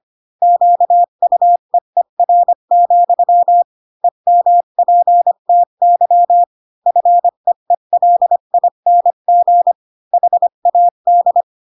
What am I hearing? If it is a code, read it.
Morse code, heard as a keyed tone, about 25 wpm, 700 Hz. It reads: QUEER, EMPTY FEELING HAD
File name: Pinocchio / Pinocchio ch5 20 25 ik1hge